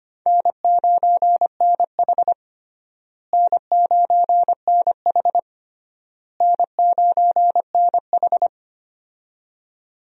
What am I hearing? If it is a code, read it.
Morse code, heard as a keyed tone, about 25 wpm, 700 Hz. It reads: N9NH N9NH N9NH
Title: USA Call Signs - 4 Characters - Repeated 3x - 25wpm